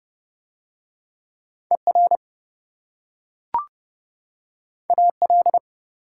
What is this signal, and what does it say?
Morse code, heard as a keyed tone, about 25 wpm, 700 Hz. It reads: ER AL
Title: Speed-Racing Prefixes and Suffixes 20wpm